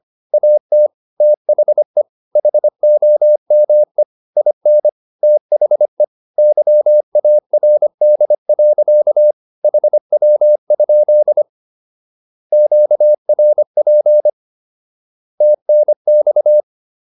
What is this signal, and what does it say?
Morse code, heard as a keyed tone, about 25 wpm, 600 Hz. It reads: AT THE HOME IN THE YARD. HW? QRP TNX
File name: SS123 25